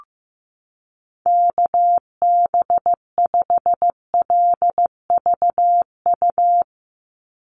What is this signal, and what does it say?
Morse code, heard as a keyed tone, about 15 wpm, 700 Hz. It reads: KB5LVU
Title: USA Call Signs 15wpm